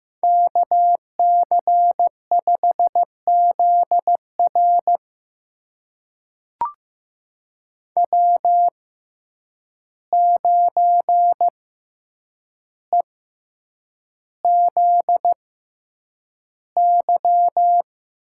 Morse code, heard as a keyed tone, about 15 wpm, spaced as for 8 wpm, 700 Hz. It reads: KC5ZR W9EZY
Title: USA Call Signs - 5 Characters - 6x Character Spacing - 15wpm